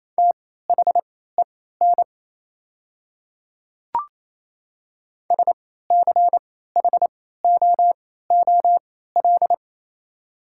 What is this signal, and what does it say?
Morse code, heard as a keyed tone, about 28 wpm, spaced as for 16 wpm, 700 Hz. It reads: THEN SCHOOL
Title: 3x Character Spacing - Top 300 Words - 28wpm